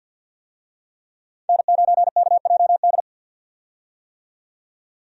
Morse code, heard as a keyed tone, about 51 wpm, 700 Hz. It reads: N9KJD
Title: USA Call Signs - 5 Characters - 50wpm